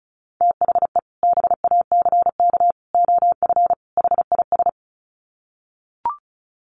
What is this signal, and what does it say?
Morse code, heard as a keyed tone, about 34 wpm, 700 Hz. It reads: THE BACK OF HIS
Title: Sets of 4 Words 35wpm